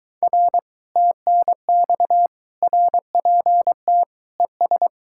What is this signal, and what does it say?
Morse code, heard as a keyed tone, about 24 wpm, 700 Hz. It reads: R TNX RPT ES